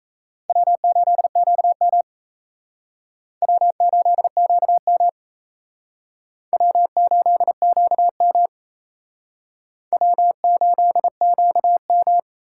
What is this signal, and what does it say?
Morse code, heard as a keyed tone, about 34 wpm, 700 Hz. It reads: W8QM W8QM W8QM W8QM
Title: Speed Racing - USA 1x2 Call Signs 28wpm